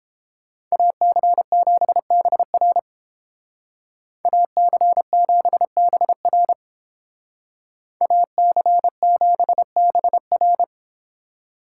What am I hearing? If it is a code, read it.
Morse code, heard as a keyed tone, about 29 wpm, 700 Hz. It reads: AC7BR AC7BR AC7BR